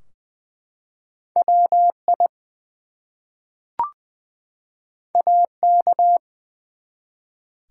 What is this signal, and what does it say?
Morse code, heard as a keyed tone, about 20 wpm, 700 Hz. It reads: WI AK
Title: US States and Canadian Provinces 20wpm